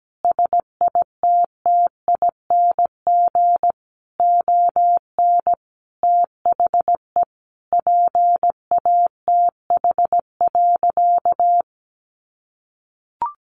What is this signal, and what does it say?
Morse code, heard as a keyed tone, about 17 wpm, 700 Hz. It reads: SITTING ON THE PATH.
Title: A Sound Of Thunder 17wpm